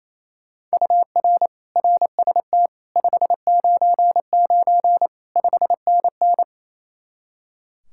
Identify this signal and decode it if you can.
Morse code, heard as a keyed tone, about 34 wpm, 700 Hz. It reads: UR RST 599 5NN
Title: Speed-Racing Common QSO Phrases Encore - 28wpm